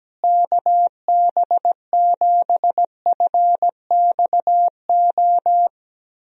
Morse code, heard as a keyed tone, about 17 wpm, 700 Hz. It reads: KB7FXO
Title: USA Call Signs - 6 Characters - Repeated 2x - 17wpm